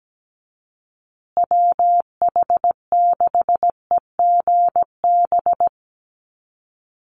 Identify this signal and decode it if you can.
Morse code, heard as a keyed tone, about 17 wpm, 700 Hz. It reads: WH6EGB
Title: USA Call Signs Repeated 2x 17wpm